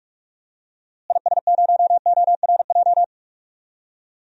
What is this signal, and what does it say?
Morse code, heard as a keyed tone, about 46 wpm, 700 Hz. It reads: IS0ORJ
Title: DXCC-100 Call Signs - 6 Characters - Repeated 2x - 45wpm